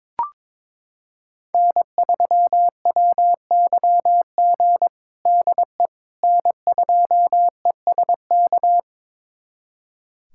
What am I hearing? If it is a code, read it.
Morse code, heard as a keyed tone, about 22 wpm, 700 Hz. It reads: N3WYG DE N2ESK
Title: USA Callsign DE Callsign - 5 Characters - 22wpm